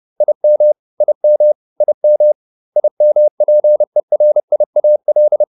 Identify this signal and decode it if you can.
Morse code, heard as a keyed tone, about 30 wpm, 600 Hz. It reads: IM IM IM IMPERIAL